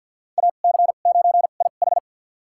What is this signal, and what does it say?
Morse code, heard as a keyed tone, about 51 wpm, spaced as for 36 wpm, 700 Hz. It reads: AC9IH
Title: USA Call Signs - 5 Characters - 2x Character Spacing - 50wpm